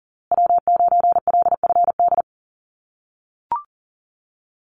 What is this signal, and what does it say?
Morse code, heard as a keyed tone, about 40 wpm, 700 Hz. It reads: W9LFD
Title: USA Call Signs Repeated 2x 40wpm